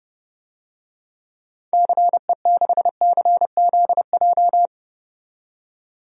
Morse code, heard as a keyed tone, about 30 wpm, 700 Hz. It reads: CE6CZJ
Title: DXCC-100 Call Signs - 6 Characters - 30wpm